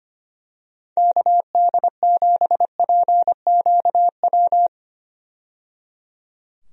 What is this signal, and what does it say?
Morse code, heard as a keyed tone, about 25 wpm, 700 Hz. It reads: KD7PQW
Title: USA Call Signs - 6 Characters - 25wpm